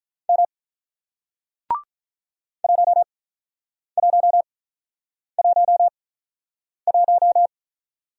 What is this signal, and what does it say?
Morse code, heard as a keyed tone, about 47 wpm, spaced as for 9 wpm, 700 Hz. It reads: M 1111